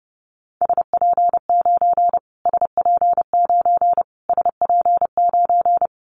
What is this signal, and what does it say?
Morse code, heard as a keyed tone, about 30 wpm, 690 Hz. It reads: SP9 SP9 SP9
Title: Callsign Prefixes for Top 25 DXCC - Repeated 3x - 30wpm